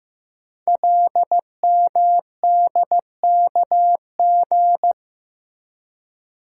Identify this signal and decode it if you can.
Morse code, heard as a keyed tone, about 15 wpm, 700 Hz. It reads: LMDKG
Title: Code Groups - 5 Letters - 15wpm